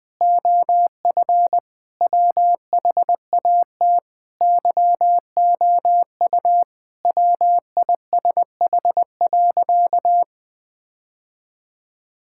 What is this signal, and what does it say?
Morse code, heard as a keyed tone, about 21 wpm, 700 Hz. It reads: OF WHAT YOU WISH.
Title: Sentences from Top 1600 Words 20wpm